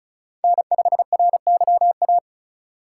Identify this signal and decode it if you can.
Morse code, heard as a keyed tone, about 36 wpm, 700 Hz. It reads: N5RYA
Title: USA Call Signs - 5 Characters - 4x Character Spacing - 35wpm